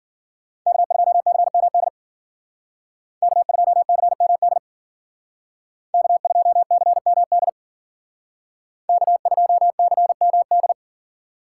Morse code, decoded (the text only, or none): K2CMD K2CMD K2CMD K2CMD